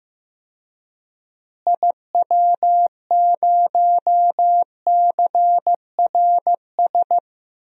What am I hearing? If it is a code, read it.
Morse code, heard as a keyed tone, about 15 wpm, 700 Hz. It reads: IW0CRS
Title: DXCC-100 Call Signs - 6 Characters - Repeated 3x - 15wpm